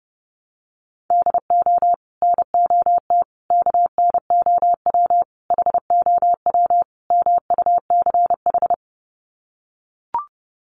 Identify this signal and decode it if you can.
Morse code, heard as a keyed tone, about 30 wpm, 700 Hz. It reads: DO NOT KNOW HOW MUCH
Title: Sets of 5 Words Encore 30wpm